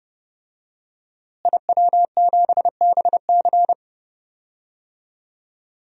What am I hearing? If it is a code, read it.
Morse code, heard as a keyed tone, about 30 wpm, 700 Hz. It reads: IW7BC
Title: DXCC-100 Call Signs - 5 Characters - Repeated 3x - 30wpm